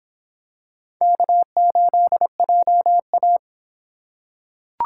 Morse code, heard as a keyed tone, about 26 wpm, 700 Hz. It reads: K8JA